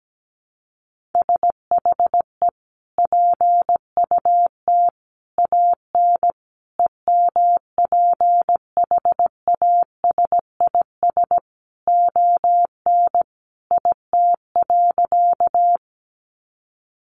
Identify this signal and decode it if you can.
Morse code, heard as a keyed tone, about 17 wpm, 700 Hz. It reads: SHE PUT AN EMPHASIS ON IT.
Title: Sentences from Top 1400 Words 17wpm